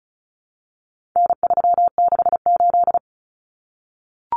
Morse code, heard as a keyed tone, about 34 wpm, 700 Hz. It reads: N368
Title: Common Cut Numbers - 4-Digit 35wpm